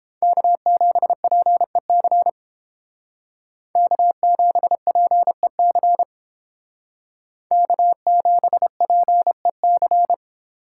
Morse code, heard as a keyed tone, about 29 wpm, 700 Hz. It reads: K7PEC K7PEC K7PEC